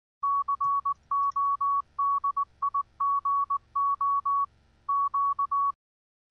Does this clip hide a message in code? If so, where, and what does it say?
Yes: Morse code from 0.2 to 5.7 s, heard as a keyed tone, about 19 wpm, 1120 Hz. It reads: CODIGO Q